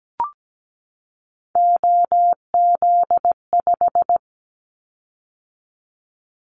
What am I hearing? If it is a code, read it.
Morse code, heard as a keyed tone, about 17 wpm, 700 Hz. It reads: OZ5
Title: Callsign Prefixes for Top 100 DXCC - 17wpm